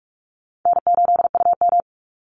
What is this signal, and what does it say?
Morse code, heard as a keyed tone, about 44 wpm, 700 Hz. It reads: N8UM
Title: Easy Call Signs Repeated 2x - 45wpm